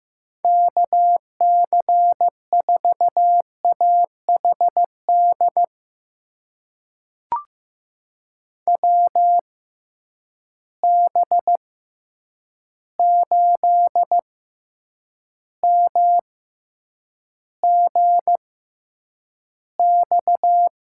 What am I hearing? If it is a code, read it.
Morse code, heard as a keyed tone, about 15 wpm, spaced as for 8 wpm, 700 Hz. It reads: KC4AHD WB8MGX
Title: USA Call Signs - 6 Characters - 6x Character Spacing - 15wpm